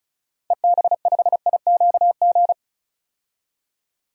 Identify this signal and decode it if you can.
Morse code, heard as a keyed tone, about 36 wpm, 700 Hz. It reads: EB5IQG
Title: DXCC-100 Call Signs - 6 Characters - 35wpm